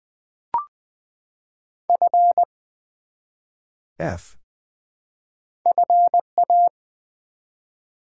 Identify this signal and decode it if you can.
Morse code, heard as a keyed tone, about 20 wpm, 700 Hz. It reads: F FA
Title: Build A Word - 1 Letter At A Time - 3 to 5-Letter Words - 20wpm